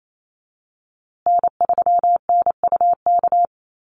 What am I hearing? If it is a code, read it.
Morse code, heard as a keyed tone, about 28 wpm, 700 Hz. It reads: N3NUK
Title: USA Call Signs Repeated 2x 28wpm